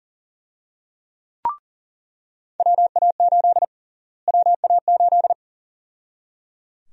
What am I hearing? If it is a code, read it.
Morse code, heard as a keyed tone, about 37 wpm, 700 Hz. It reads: WA8 WA8
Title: USA Call Sign Prefixes - Repeated 2x - 40wpm